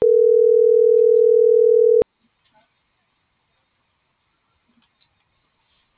Ringback tone (North America), 440 + 480 Hz.